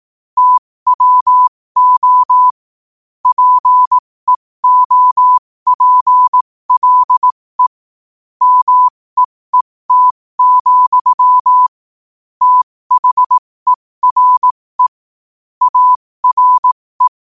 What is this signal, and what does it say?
Morse code, heard as a keyed tone, about 18 wpm, spaced as for 15 wpm, 1000 Hz. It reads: TWO PEOPLE MEET, THERE ARE
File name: qotdfiles / 15wpm